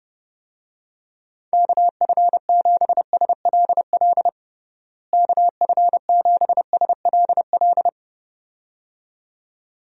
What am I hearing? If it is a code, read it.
Morse code, heard as a keyed tone, about 30 wpm, 700 Hz. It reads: KF7SLL KF7SLL
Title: USA Call Signs - 6 Characters - Repeated 2x - 30wpm